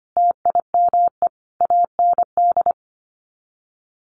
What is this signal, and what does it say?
Morse code, heard as a keyed tone, about 25 wpm, 700 Hz. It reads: TIME AND
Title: Sets of 5 Words Encore2 25wpm